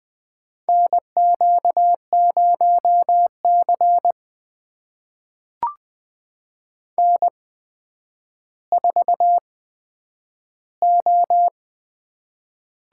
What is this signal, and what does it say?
Morse code, heard as a keyed tone, about 20 wpm, spaced as for 10 wpm, 700 Hz. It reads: NQ0C N4O
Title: USA Call Signs - 4 Characters - 8x Character Spacing - 20wpm